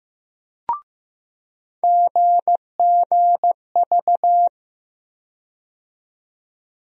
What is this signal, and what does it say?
Morse code, heard as a keyed tone, about 16 wpm, 700 Hz. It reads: GGV
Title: USA Call Sign Postfixes - 3 Letters 15wpm